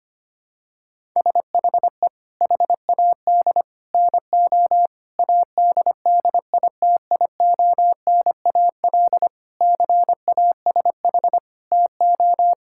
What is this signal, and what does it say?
Morse code, heard as a keyed tone, about 25 wpm, 700 Hz. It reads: SHE HAD NO ADDITIONAL CASH TO
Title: Sentences from Top 1000 Words - Review Part 6 - 25wpm